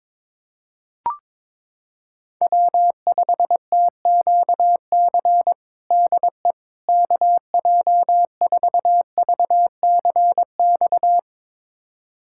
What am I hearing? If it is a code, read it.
Morse code, heard as a keyed tone, about 22 wpm, 700 Hz. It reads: W5TQC DE KJ4VCX
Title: USA Callsign DE Callsign 22wpm